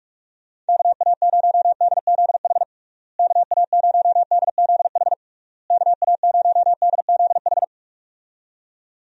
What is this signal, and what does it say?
Morse code, heard as a keyed tone, about 46 wpm, 700 Hz. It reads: KA0DZH KA0DZH KA0DZH
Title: USA Call Signs - 6 Characters - Repeated 3x - 45wpm